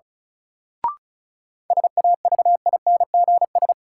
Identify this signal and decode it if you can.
Morse code, heard as a keyed tone, about 36 wpm, 700 Hz. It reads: SAVINGS
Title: Sentences from Top 1200 Words - New Word Drill Alt - 35wpm